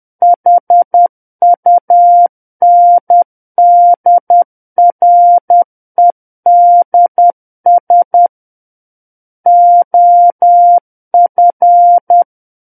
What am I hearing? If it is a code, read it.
Morse code, heard as a keyed tone, about 10 wpm, 700 Hz. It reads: HUNDREDS OF
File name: Book One / 01-Chapter 1 - 10 WPM